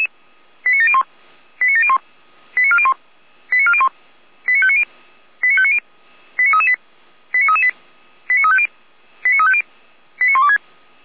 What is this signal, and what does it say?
Five-tone selective call (ZVEI), heard as quick runs of five tones, tones of about 80 ms. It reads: 89701 89701 89401 89401 89590 89590 89338 89338 89350 89350 89126